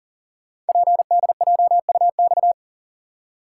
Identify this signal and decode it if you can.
Morse code, heard as a keyed tone, about 39 wpm, 700 Hz. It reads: PDJUX
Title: Code Groups - 5 Letters - 40wpm